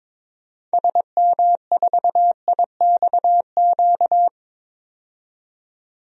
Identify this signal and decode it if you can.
Morse code, heard as a keyed tone, about 22 wpm, 700 Hz. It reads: SM4IXQ
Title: DXCC-100 Call Signs - 6 Characters - 22wpm